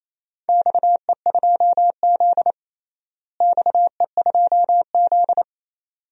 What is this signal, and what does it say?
Morse code, heard as a keyed tone, about 28 wpm, 700 Hz. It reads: XE2Z XE2Z
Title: DXCC-100 Call Signs - 4 Characters - Repeated 2x - 28wpm